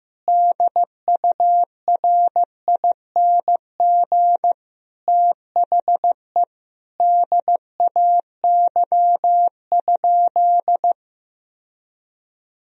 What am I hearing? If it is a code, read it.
Morse code, heard as a keyed tone, about 15 wpm, 700 Hz. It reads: DURING THE DAY?
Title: Sentences from Top 1000 Words - Review Part 2 - 15wpm